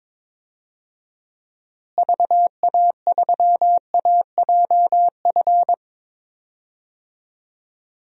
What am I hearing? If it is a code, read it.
Morse code, heard as a keyed tone, about 22 wpm, 700 Hz. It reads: VA3AJF
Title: DXCC-100 Call Signs - 6 Characters - Repeated 3x - 22wpm